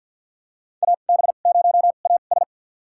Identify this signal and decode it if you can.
Morse code, heard as a keyed tone, about 50 wpm, spaced as for 36 wpm, 700 Hz. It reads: AB0AS
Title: USA Call Signs - 5 Characters - 2x Character Spacing - 50wpm